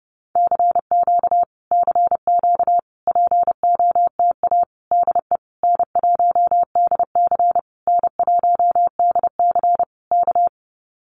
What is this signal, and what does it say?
Morse code, heard as a keyed tone, about 30 wpm, 700 Hz. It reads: CQ CQ POTA DE N1DC N1DC K
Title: Common QSO Phrases Encore 30wpm